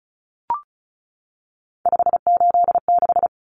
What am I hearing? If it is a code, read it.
Morse code, heard as a keyed tone, about 34 wpm, 700 Hz. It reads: 586
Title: Common Cut Numbers - 3-Digit 35wpm